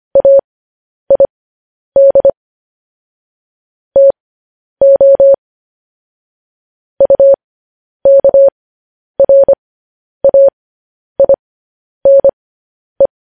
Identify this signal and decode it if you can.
Morse code, heard as a keyed tone, about 25 wpm, spaced as for 10 wpm, 550 Hz. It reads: AID TO UKRAINE